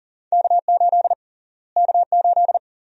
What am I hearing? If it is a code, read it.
Morse code, heard as a keyed tone, about 37 wpm, 700 Hz. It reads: K8 K8
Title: USA Call Sign Prefixes - Repeated 2x - 40wpm